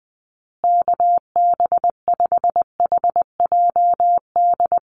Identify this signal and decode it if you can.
Morse code, heard as a keyed tone, about 19 wpm, 700 Hz. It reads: KB5HJD
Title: USA Call Signs Repeated 2x 20wpm